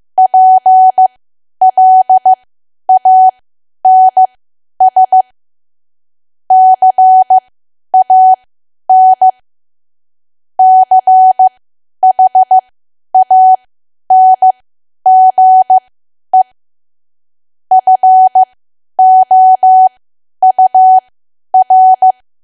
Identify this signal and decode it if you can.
Morse code, heard as a keyed tone, about 15 wpm, spaced as for 10 wpm, 750 Hz. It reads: PLANS CAN CHANGE FOUR
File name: Archive / 10 WPM / 10WPM